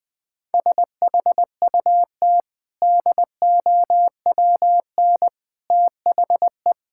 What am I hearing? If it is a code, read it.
Morse code, heard as a keyed tone, about 20 wpm, 700 Hz. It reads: SHUT DOWN THE
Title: Easy Sentences with Hints - 20wpm